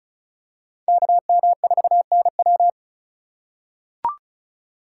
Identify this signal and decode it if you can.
Morse code, heard as a keyed tone, about 36 wpm, 700 Hz. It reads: KM4NW
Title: USA Call Signs - 5 Characters - 2x Character Spacing - 35wpm